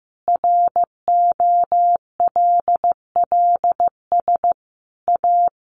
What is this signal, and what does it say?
Morse code, heard as a keyed tone, about 15 wpm, 700 Hz. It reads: ROLLS A
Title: Easy Phrases 15wpm